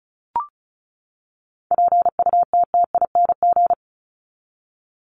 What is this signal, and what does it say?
Morse code, heard as a keyed tone, about 34 wpm, 700 Hz. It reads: PUTTING
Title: ING Suffix - Top 100 Words with Base Words 35wpm